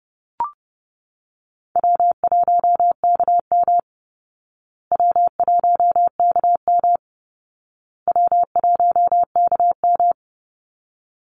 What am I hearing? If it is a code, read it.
Morse code, heard as a keyed tone, about 30 wpm, 700 Hz. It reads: W1KM W1KM W1KM
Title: Call Signs Repeated 2x - 30wpm